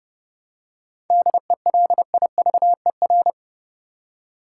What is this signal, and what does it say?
Morse code, heard as a keyed tone, about 30 wpm, 700 Hz. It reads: DELIVER
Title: Seven Letter Words - 30wpm